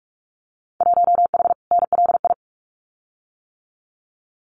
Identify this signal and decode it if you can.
Morse code, heard as a keyed tone, about 46 wpm, 700 Hz. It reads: 1H NLI
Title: Winter Field Day Exchanges 45wpm